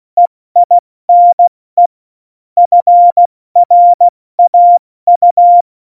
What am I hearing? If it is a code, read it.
Morse code, heard as a keyed tone, about 16 wpm, spaced as for 14 wpm, 700 Hz. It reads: EINE FRAU
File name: Morsekurs / Witze / Witz 23. April 2019, 70 F80